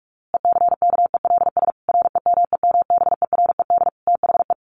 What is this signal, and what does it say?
Morse code, heard as a keyed tone, about 44 wpm, 700 Hz. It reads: ECKELS REMEMBERED THE
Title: A Sound Of Thunder 45wpm